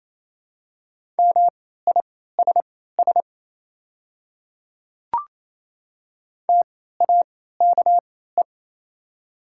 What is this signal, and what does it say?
Morse code, heard as a keyed tone, about 28 wpm, spaced as for 16 wpm, 700 Hz. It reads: MISS TAKE
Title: 3x Character Spacing - Top 500 Words - 28wpm